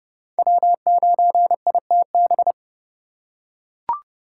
Morse code, heard as a keyed tone, about 30 wpm, 700 Hz. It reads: W9ITB
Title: USA Call Signs - 4x Character Spacing - 30wpm